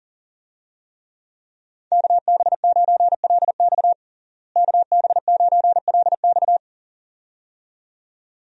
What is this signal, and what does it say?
Morse code, heard as a keyed tone, about 40 wpm, 700 Hz. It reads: KB9LX KB9LX
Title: USA Call Signs - 5 Characters - Repeated 2x - 40wpm